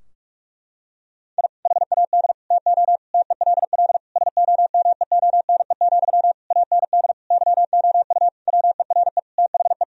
Morse code, heard as a keyed tone, about 41 wpm, 700 Hz. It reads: I HAD TO TELL SOMEONE, AND YOU WERE THE
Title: Sentences from Top 1000 Words - Review Part 4 - 45wpm